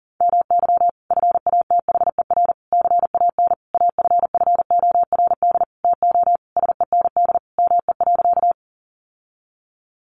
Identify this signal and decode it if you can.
Morse code, heard as a keyed tone, about 37 wpm, 700 Hz. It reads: MY FATHER CAN AFFORD TO SEND ME.
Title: Sentences from Top 1500 Words 40wpm